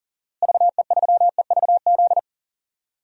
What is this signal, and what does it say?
Morse code, heard as a keyed tone, about 40 wpm, 700 Hz. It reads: VE3EVZ
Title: DXCC-100 Call Signs - 6 Characters - Repeated 3x - 40wpm